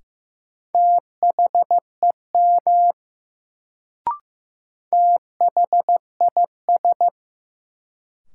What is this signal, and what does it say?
Morse code, heard as a keyed tone, about 15 wpm, 700 Hz. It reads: THEM THIS
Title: Sets of 2 Words - New Word Drill - 15wpm